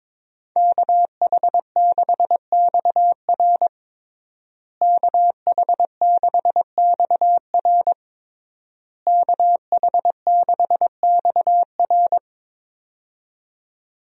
Morse code, heard as a keyed tone, about 22 wpm, 700 Hz. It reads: KH6XR KH6XR KH6XR
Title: DXCC-100 Call Signs - 5 Characters - Repeated 3x - 22wpm